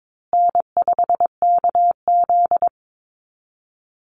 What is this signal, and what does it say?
Morse code, heard as a keyed tone, about 22 wpm, 690 Hz. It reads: N5KZ
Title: USA Call Signs Repeated 3x 22wpm